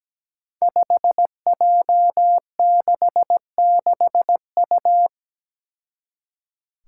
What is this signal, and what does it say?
Morse code, heard as a keyed tone, about 17 wpm, 700 Hz. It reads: 5J66U
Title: Five Challenging Letter-Number Combinations 17wpm